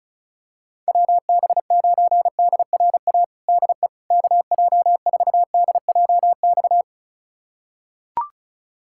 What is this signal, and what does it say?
Morse code, heard as a keyed tone, about 36 wpm, 700 Hz. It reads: WB9DRA DE KJ4DJX
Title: USA Callsign DE Callsign - 6 Characters - 35wpm